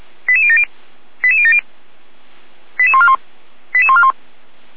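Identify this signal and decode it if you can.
Five-tone selective call (ZVEI), heard as quick runs of five tones, tones of about 70 ms. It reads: 88070 88070 88141 88141
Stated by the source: ZVEI five-tone call 88090 received via SDR, Feuerwehr Orlamünde.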